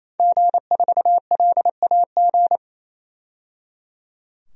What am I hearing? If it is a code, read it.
Morse code, heard as a keyed tone, about 28 wpm, 700 Hz. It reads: G4LAG
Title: DXCC-100 Call Signs - 5 Characters - 28wpm